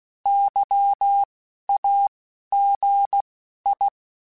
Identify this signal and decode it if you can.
Morse code, heard as a keyed tone, about 16 wpm, spaced as for 11 wpm, 800 Hz. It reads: YAGI
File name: fromSMs / FromSMs mp3s / Antennas@10